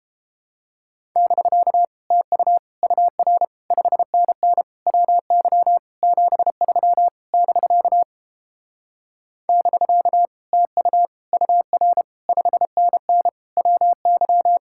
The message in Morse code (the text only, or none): <BK> TU UR 5NN WY 73 <BK> <BK> TU UR 5NN WY